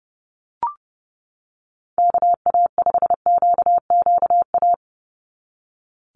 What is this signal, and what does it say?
Morse code, heard as a keyed tone, about 30 wpm, 700 Hz. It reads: KA5QQA
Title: NA SOTA Chaser Callsigns 30wpm